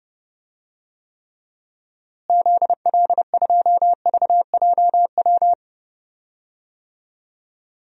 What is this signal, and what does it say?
Morse code, heard as a keyed tone, about 30 wpm, 700 Hz. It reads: ZL2VJW
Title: DXCC-100 Call Signs - 6 Characters - Repeated 3x - 30wpm